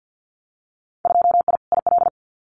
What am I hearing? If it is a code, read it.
Morse code, heard as a keyed tone, about 50 wpm, 700 Hz. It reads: 2I IL
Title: Winter Field Day Exchanges 50wpm